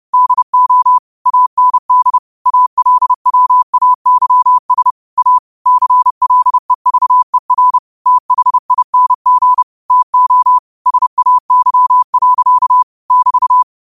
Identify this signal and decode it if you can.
Morse code, heard as a keyed tone, about 30 wpm, 1000 Hz. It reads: DO AND ALWAYS A CLEVER THING TO SAY. <BT>
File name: qotdfiles / 30wpm